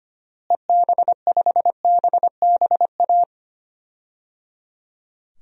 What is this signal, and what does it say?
Morse code, heard as a keyed tone, about 25 wpm, 700 Hz. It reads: EB5BBA
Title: DXCC-100 Call Signs - 6 Characters - Repeated 3x - 25wpm